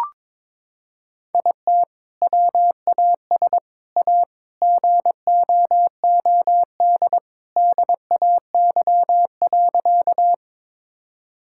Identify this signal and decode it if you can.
Morse code, heard as a keyed tone, about 22 wpm, 700 Hz. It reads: IT WAS A GOOD DAY.
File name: practice / 22wpm / Sentences with 2-Word Phrases - 22wpm